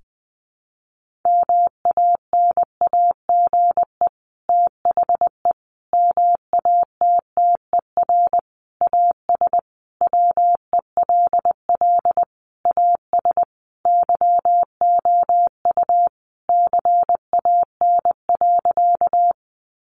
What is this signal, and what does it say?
Morse code, heard as a keyed tone, about 19 wpm, 700 Hz. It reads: MANAGE THE MATTER AS WELL AS YOU CAN.
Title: Sentences from Top 1400 Words 20wpm